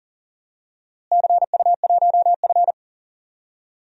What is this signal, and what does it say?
Morse code, heard as a keyed tone, about 40 wpm, 700 Hz. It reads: CU1F
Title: DXCC-100 Call Signs - 4 Characters - Repeated 2x - 40wpm